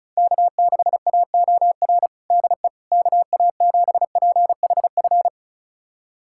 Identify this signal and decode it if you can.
Morse code, heard as a keyed tone, about 34 wpm, 700 Hz. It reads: K6AOR DE KA7PHF
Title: USA Callsign DE Callsign 35wpm